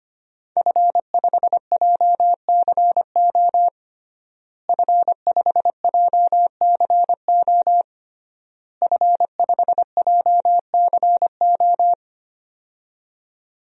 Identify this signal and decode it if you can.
Morse code, heard as a keyed tone, about 25 wpm, 700 Hz. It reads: F5JCO F5JCO F5JCO
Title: DXCC-100 Call Signs - 5 Characters - Repeated 3x - 25wpm